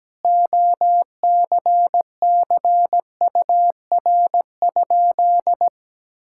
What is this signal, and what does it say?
Morse code, heard as a keyed tone, about 17 wpm, 700 Hz. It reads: OCCUR?
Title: Sentences from Top 1000 Words - Review Part 6 - 17wpm